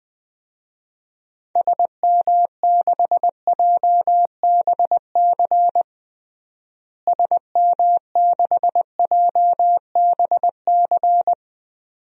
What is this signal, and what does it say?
Morse code, heard as a keyed tone, about 21 wpm, 700 Hz. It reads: SM6JBC SM6JBC
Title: DXCC-100 Call Signs - 6 Characters - Repeated 2x - 20wpm